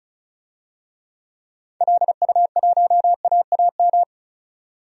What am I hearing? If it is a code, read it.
Morse code, heard as a keyed tone, about 34 wpm, 700 Hz. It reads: LU1AAM
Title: DXCC-100 Call Signs - 6 Characters - 35wpm